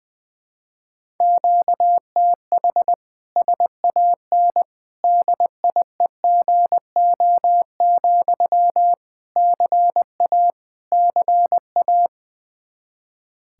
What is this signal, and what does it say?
Morse code, heard as a keyed tone, about 20 wpm, 700 Hz. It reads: QTH SAN DIEGO, CA CA
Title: Practice QSO - Exchange 1 - Beginning - 20wpm